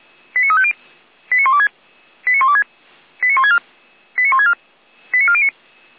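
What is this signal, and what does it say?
Five-tone selective call (ZVEI), heard as quick runs of five tones, tones of about 70 ms. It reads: 89350 89126 89126 89164 89164 89409
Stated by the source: ZVEI five-tone call received by SDR.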